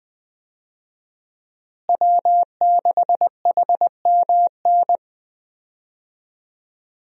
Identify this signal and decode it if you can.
Morse code, heard as a keyed tone, about 20 wpm, 700 Hz. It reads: W6HMN
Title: USA Call Signs - 5 Characters - Repeated 3x - 20wpm